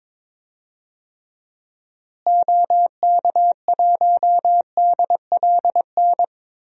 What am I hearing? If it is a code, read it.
Morse code, heard as a keyed tone, about 22 wpm, 700 Hz. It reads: OK1DLN
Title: DXCC-100 Call Signs - 6 Characters - Repeated 3x - 22wpm